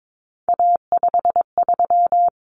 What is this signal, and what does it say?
Morse code, heard as a keyed tone, about 22 wpm, 700 Hz. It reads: A53
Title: Common Cut Numbers - 3-Digit 22wpm